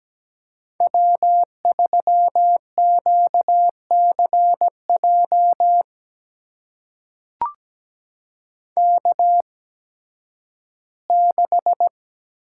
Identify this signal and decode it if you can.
Morse code, heard as a keyed tone, about 17 wpm, spaced as for 11 wpm, 700 Hz. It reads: W3QCJ K6
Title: USA Call Signs - 5 Characters - 8x Character Spacing - 17wpm